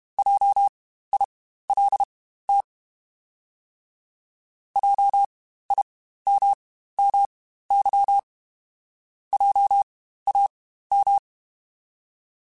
Morse code, heard as a keyed tone, about 32 wpm, spaced as for 15 wpm, 800 Hz. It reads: JILT JIMMY JAM
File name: Spaced Alphas / Spaced Alphas mp3s / Js sp @30